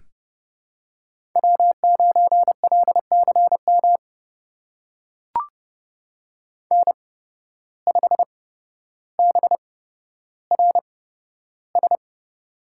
Morse code, heard as a keyed tone, about 30 wpm, spaced as for 13 wpm, 700 Hz. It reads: W9LCM N5BRS
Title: USA Call Signs - 5 Characters - 8x Character Spacing - 30wpm